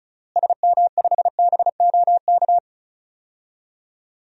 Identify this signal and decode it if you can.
Morse code, heard as a keyed tone, about 36 wpm, 700 Hz. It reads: SM5BOK
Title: DXCC-100 Call Signs - 6 Characters - Repeated 3x - 35wpm